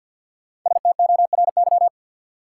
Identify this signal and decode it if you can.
Morse code, heard as a keyed tone, about 53 wpm, 700 Hz. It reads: STORY
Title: Sentences from Top 300 Words - New Word Drill - 50wpm